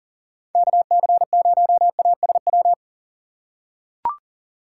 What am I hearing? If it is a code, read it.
Morse code, heard as a keyed tone, about 40 wpm, 700 Hz. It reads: KC0ASW
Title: USA Call Signs - 2x Character Spacing - 40wpm